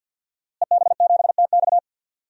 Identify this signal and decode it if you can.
Morse code, heard as a keyed tone, about 45 wpm, 700 Hz. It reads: EB7TX
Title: DXCC-100 Call Signs - 5 Characters - Repeated 2x - 50wpm